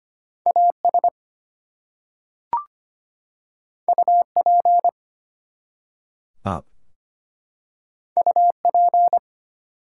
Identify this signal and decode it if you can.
Morse code, heard as a keyed tone, about 25 wpm, 700 Hz. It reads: AS UP UP
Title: Two Letter Words - 25wpm